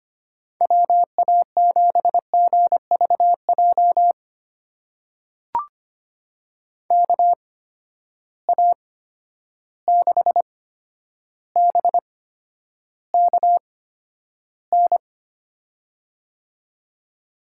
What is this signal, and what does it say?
Morse code, heard as a keyed tone, about 25 wpm, spaced as for 11 wpm, 700 Hz. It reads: WA7GVJ KA6BKN
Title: USA Call Signs - 6 Characters - 8x Character Spacing - 25wpm